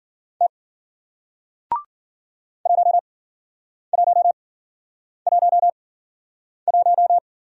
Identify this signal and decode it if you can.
Morse code, heard as a keyed tone, about 55 wpm, spaced as for 9 wpm, 700 Hz. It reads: T 1111